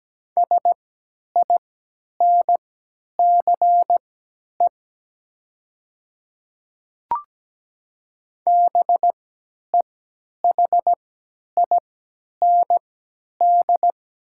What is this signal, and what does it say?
Morse code, heard as a keyed tone, about 17 wpm, spaced as for 10 wpm, 700 Hz. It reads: SINCE BEHIND
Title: 3x Character Spacing - Top 500 Words - 17wpm